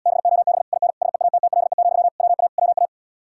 Morse code, heard as a keyed tone, about 80 wpm, 700 Hz. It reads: BOB IN SEATTLE, WA WA